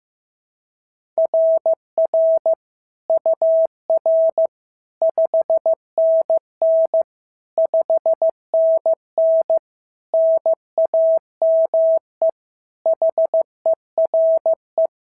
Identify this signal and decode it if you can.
Morse code, heard as a keyed tone, about 14 wpm, 650 Hz. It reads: RR UR 5NN 5NN NAME HERE
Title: SOTA QSO Practice 15wpm